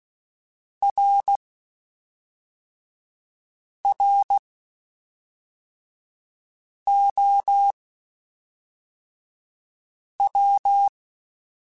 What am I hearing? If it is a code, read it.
Morse code, heard as a keyed tone, about 16 wpm, spaced as for 3 wpm, 800 Hz. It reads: RROW
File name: Spaced / Chuckles Spaced-mp3s / MiddleAge sp @5